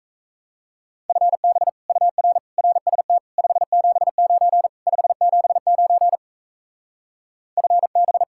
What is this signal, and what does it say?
Morse code, heard as a keyed tone, about 34 wpm, 700 Hz. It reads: FB UR RST 579 579 FB
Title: Speed-Racing Common QSO Phrases - 28wpm